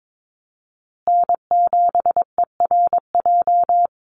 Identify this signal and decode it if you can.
Morse code, heard as a keyed tone, about 22 wpm, 700 Hz. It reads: N7ERJ
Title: USA Call Signs Repeated 2x 22wpm